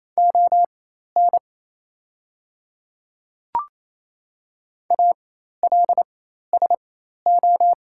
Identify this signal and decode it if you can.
Morse code, heard as a keyed tone, about 28 wpm, spaced as for 13 wpm, 700 Hz. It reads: ON ALSO
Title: 4x Character Spacing - Top 200 Words - 28wpm